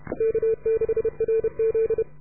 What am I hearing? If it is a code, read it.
Morse code, heard as a keyed tone, about 31 wpm, 440 Hz. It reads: K6RZ